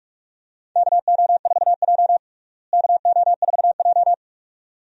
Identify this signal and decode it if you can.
Morse code, heard as a keyed tone, about 46 wpm, 700 Hz. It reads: KO4J KO4J
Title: USA Call Signs - 4 Characters - Repeated 2x - 45wpm